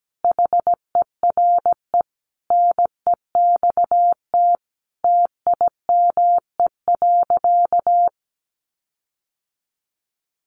Morse code, heard as a keyed tone, about 17 wpm, 700 Hz. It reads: HERE NEXT TIME.
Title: Sentences from Top 1500 Words 17wpm